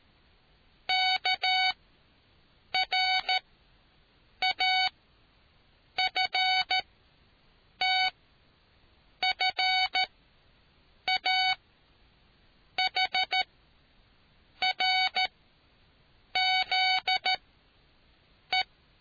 Morse code, heard as a keyed tone, about 13 wpm, spaced as for 6 wpm, 750 Hz. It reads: KRAFTFAHRZE